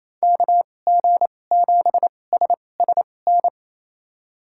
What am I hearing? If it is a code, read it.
Morse code, heard as a keyed tone, about 28 wpm, spaced as for 20 wpm, 700 Hz. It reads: KG7SSN
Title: USA Call Signs - 6 Characters - 2x Character Spacing - 28wpm